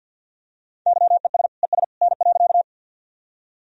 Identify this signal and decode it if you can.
Morse code, heard as a keyed tone, about 48 wpm, 700 Hz. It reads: YES ES N.
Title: Sentences from Top 500 Words and Abbreviations 50wpm